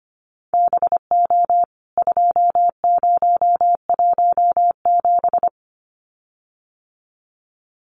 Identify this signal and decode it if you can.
Morse code, heard as a keyed tone, about 25 wpm, 700 Hz. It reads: BO 2017
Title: CWT Exchange 25wpm